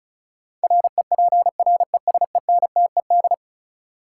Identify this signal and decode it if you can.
Morse code, heard as a keyed tone, about 36 wpm, 700 Hz. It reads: REPRESENTED
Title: Sets of 5 Words - New Word Drill - 35wpm